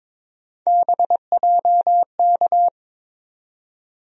Morse code, heard as a keyed tone, about 22 wpm, 700 Hz. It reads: BJK
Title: USA Call Sign Postfixes - 3 Letters 22wpm